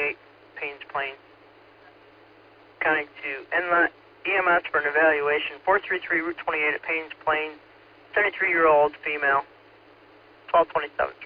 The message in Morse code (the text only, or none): none